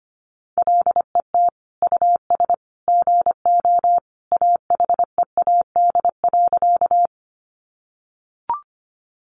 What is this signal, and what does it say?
Morse code, heard as a keyed tone, about 25 wpm, 700 Hz. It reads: LET US GO AHEAD.
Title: Sentences from Top 1100 Words 25wpm